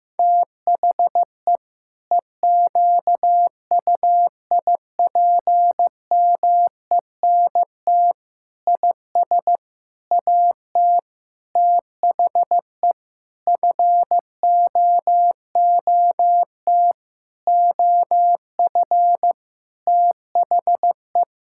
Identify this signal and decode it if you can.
Morse code, heard as a keyed tone, about 15 wpm, 700 Hz. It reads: THE EQUIPMENT IS AT THE FOOT OF THE
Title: Sentences from Top 1000 Words - Review Part 5 - 15wpm